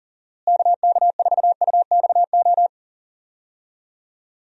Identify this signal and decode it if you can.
Morse code, heard as a keyed tone, about 38 wpm, 700 Hz. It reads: KK4UXO
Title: USA Call Signs - 6 Characters - Repeated 3x - 40wpm